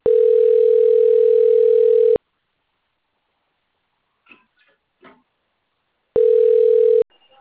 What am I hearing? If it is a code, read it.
Ringback tone (North America), 440 + 480 Hz.